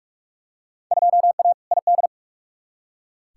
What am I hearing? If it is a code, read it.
Morse code, heard as a keyed tone, about 44 wpm, 700 Hz. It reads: 2A ID
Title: Field Day Exchanges - 45wpm